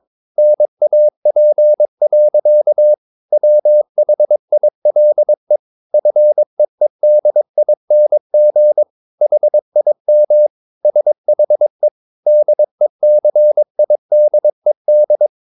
Morse code, heard as a keyed tone, about 22 wpm, 600 Hz. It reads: NAP. WHILE FEEDING HIM SHE DECIDED
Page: Romanen Never af Ken Follet på engelsk.